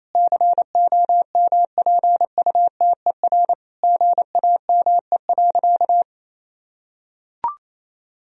Morse code, heard as a keyed tone, about 28 wpm, 700 Hz. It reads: COMPUTER GAME.
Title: Sentences from Top 1000 Words - Review Part 3 - 28wpm